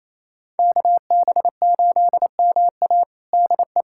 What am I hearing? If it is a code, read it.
Morse code, heard as a keyed tone, about 27 wpm, 700 Hz. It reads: KB8MA DE
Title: USA Callsign DE Callsign - 5 Characters - 28wpm